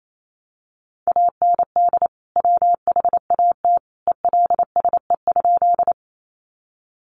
Morse code, heard as a keyed tone, about 28 wpm, 700 Hz. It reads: AND WHAT ELSE?
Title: Sentences from Top 800 Words 28wpm